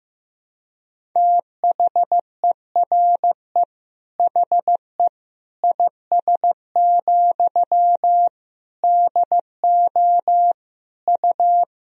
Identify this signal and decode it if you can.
Morse code, heard as a keyed tone, about 15 wpm, 700 Hz. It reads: THERE HE IS, DO U
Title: Sentences from Top 100 Words and Abbreviations 15wpm